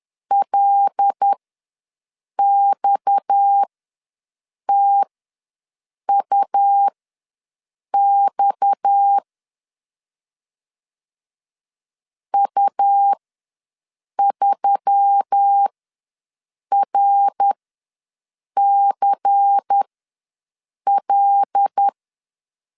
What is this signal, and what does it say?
Morse code, heard as a keyed tone, about 10 wpm, spaced as for 6 wpm, 800 Hz. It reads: LXTUX U3RCL
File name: LEZIONI / 6 PKX0 / 6 PKX ws 10wpm 3706